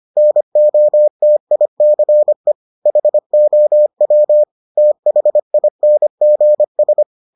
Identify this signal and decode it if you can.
Morse code, heard as a keyed tone, about 25 wpm, 600 Hz. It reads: NOTICE HOW THINGS